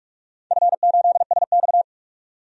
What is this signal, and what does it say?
Morse code, heard as a keyed tone, about 46 wpm, 700 Hz. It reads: F8SX
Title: DXCC-100 Call Signs - 4 Characters - 45wpm